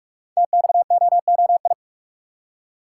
Morse code, heard as a keyed tone, about 44 wpm, 700 Hz. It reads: TXOOI
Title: Code Groups - 5 Letters - 45wpm